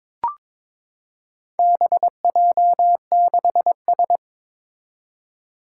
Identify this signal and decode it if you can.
Morse code, heard as a keyed tone, about 22 wpm, 700 Hz. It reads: BJ6S
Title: Four Challenging Letter-Number Combinations 22wpm